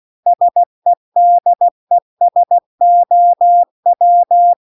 Morse code, heard as a keyed tone, about 16 wpm, 700 Hz. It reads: SEDESOW